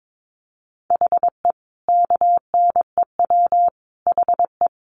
Morse code, heard as a keyed tone, about 22 wpm, 700 Hz. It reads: HE KNEW HE
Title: Sentences from Top 900 Words 22wpm